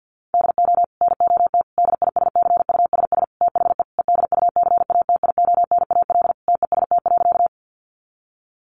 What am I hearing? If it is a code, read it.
Morse code, heard as a keyed tone, about 50 wpm, 700 Hz. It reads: DO NOT DISCUSS THE EDUCATIONAL TEST.
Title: Sentences from Top 1200 Words 50wpm